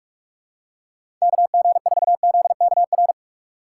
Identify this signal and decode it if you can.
Morse code, heard as a keyed tone, about 47 wpm, 700 Hz. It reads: KG4ZKR
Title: USA Call Signs - 6x Character Spacing - 45wpm